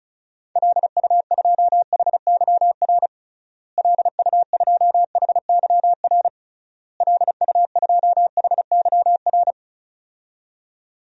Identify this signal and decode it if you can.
Morse code, heard as a keyed tone, about 36 wpm, 700 Hz. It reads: LU2HYR LU2HYR LU2HYR
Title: DXCC-100 Call Signs - 6 Characters - Repeated 3x - 35wpm